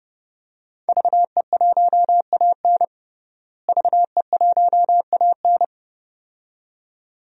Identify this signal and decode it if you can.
Morse code, heard as a keyed tone, about 32 wpm, 700 Hz. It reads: VE1AN VE1AN
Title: DXCC-100 Call Signs - 5 Characters - Repeated 2x - 30wpm